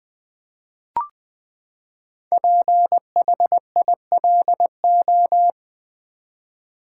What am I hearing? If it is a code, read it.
Morse code, heard as a keyed tone, about 20 wpm, 700 Hz. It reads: PHILO
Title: Build A Word - 5 Letters At A Time - 10-Letter Words - 20wpm